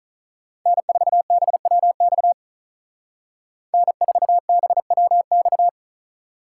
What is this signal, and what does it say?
Morse code, heard as a keyed tone, about 39 wpm, 700 Hz. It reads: N4BWX N4BWX
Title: Speed Racing - USA Call Signs 35wpm